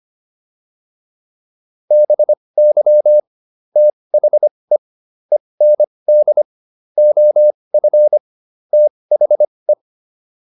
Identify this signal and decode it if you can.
Morse code, heard as a keyed tone, about 25 wpm, spaced as for 20 wpm, 600 Hz. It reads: BY THE END OF THE